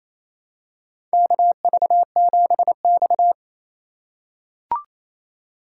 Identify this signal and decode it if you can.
Morse code, heard as a keyed tone, about 28 wpm, 700 Hz. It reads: KV7X
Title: USA Call Signs - 4 Characters - 4x Character Spacing - 28wpm